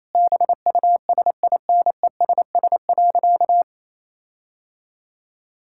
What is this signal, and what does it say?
Morse code, heard as a keyed tone, about 28 wpm, 700 Hz. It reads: BUSINESS.
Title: Sentences from Top 1600 Words 28wpm